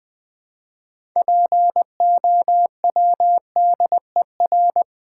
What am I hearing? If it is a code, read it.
Morse code, heard as a keyed tone, about 20 wpm, 700 Hz. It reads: POWDER
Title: Build A Word - 2 Letters At A Time - 6-Letter Words - 20wpm